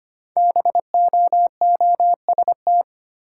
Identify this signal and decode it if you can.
Morse code, heard as a keyed tone, about 25 wpm, 700 Hz. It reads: BOOST
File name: 5 Letter Words - First 3 Letters Given - 25wpm